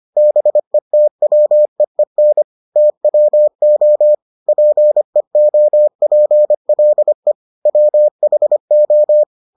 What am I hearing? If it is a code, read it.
Morse code, heard as a keyed tone, about 25 wpm, 600 Hz. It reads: BETWEEN TWO PEOPLE WHO